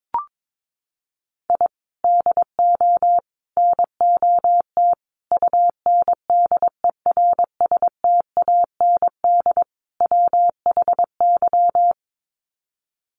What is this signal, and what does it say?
Morse code, heard as a keyed tone, about 22 wpm, 700 Hz. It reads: I DO NOT UNDERSTAND WHY
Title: Sets of 5 Words Encore 22wpm